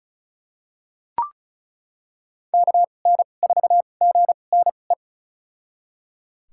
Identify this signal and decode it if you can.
Morse code, heard as a keyed tone, about 34 wpm, spaced as for 25 wpm, 700 Hz. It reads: KN4GNE
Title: USA Call Signs - 6 Characters - 2x Character Spacing - 35wpm